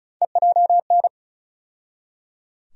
Morse code, heard as a keyed tone, about 34 wpm, 700 Hz. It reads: EJN
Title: USA Call Sign Postfixes - 3 Letters 35wpm